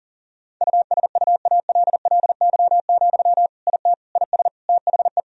Morse code, heard as a keyed tone, about 40 wpm, 700 Hz. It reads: USUALLY, IT IS THE
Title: Sentences from Top 1000 Words - Review Part 3 - 40wpm